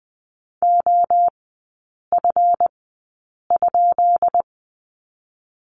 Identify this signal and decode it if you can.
Morse code, heard as a keyed tone, about 20 wpm, spaced as for 8 wpm, 690 Hz. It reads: OF?
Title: 212 Lesson 34e - Number 8 - Three Characters 20wpm